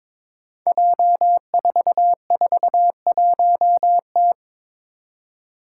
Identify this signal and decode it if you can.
Morse code, heard as a keyed tone, about 22 wpm, 700 Hz. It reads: J441T
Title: Five Challenging Letter-Number Combinations 22wpm